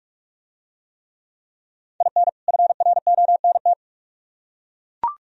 Morse code, heard as a keyed tone, about 42 wpm, 700 Hz. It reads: IN FRONT